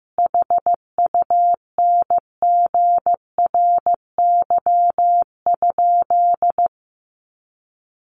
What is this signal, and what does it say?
Morse code, heard as a keyed tone, about 15 wpm, 700 Hz. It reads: HUNGRY?
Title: Sentences from Top 1400 Words 15wpm